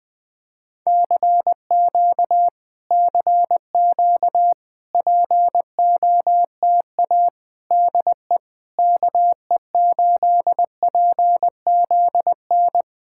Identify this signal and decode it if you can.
Morse code, heard as a keyed tone, about 20 wpm, 690 Hz. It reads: CQ CQ POTA DE KE8PZN
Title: POTA QSO Practice - 20wpm